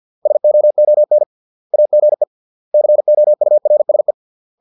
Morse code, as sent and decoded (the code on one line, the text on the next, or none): ... --- --- -. / .- --. / -.-. --- ..- .-. ... .
SOON AGE COURSE